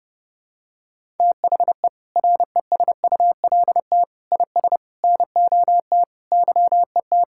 Morse code, heard as a keyed tone, about 30 wpm, 700 Hz. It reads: THE RESULT IS NOT YET
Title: Sentences from Top 1000 Words - Review Part 5 - 30wpm